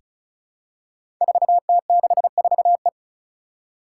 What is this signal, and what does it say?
Morse code, heard as a keyed tone, about 36 wpm, 700 Hz. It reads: 4T64E
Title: Five Challenging Letter-Number Combinations 35wpm